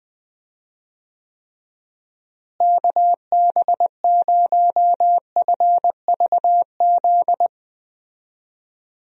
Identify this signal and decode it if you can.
Morse code, heard as a keyed tone, about 21 wpm, 700 Hz. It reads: KB0FVZ
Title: USA Call Signs - 6 Characters - Repeated 3x - 20wpm